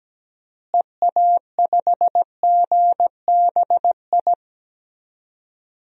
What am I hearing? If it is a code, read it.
Morse code, heard as a keyed tone, about 17 wpm, 700 Hz. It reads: EA5GBI
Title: DXCC-100 Call Signs - 6 Characters - Repeated 3x - 17wpm